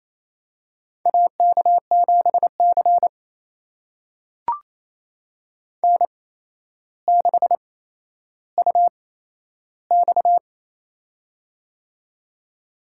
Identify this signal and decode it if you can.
Morse code, heard as a keyed tone, about 27 wpm, spaced as for 12 wpm, 700 Hz. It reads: AK7C N6UX
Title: USA Call Signs - 4 Characters - 8x Character Spacing - 28wpm